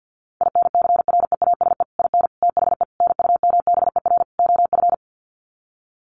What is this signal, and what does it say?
Morse code, heard as a keyed tone, about 50 wpm, 700 Hz. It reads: INCREASE IN THE NUMBER OF
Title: Sets of 5 Words 50wpm